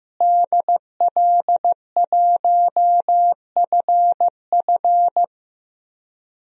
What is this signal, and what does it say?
Morse code, heard as a keyed tone, about 15 wpm, 700 Hz. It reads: DL1FF
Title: DXCC-100 Call Signs - 5 Characters - Repeated 2x - 15wpm